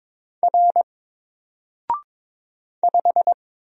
Morse code, heard as a keyed tone, about 22 wpm, 700 Hz. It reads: R 5
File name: Single-Letter-Number-22wpm